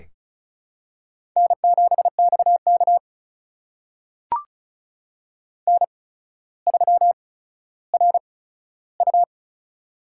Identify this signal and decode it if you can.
Morse code, heard as a keyed tone, about 35 wpm, spaced as for 15 wpm, 700 Hz. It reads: N7XK N3RU
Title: USA Call Signs - 4 Characters - 8x Character Spacing - 35wpm